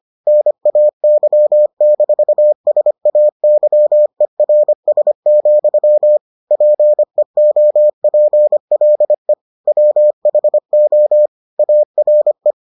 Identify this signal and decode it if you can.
Morse code, heard as a keyed tone, about 25 wpm, 600 Hz. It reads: NAY-SAYERS, PEOPLE WHO ARE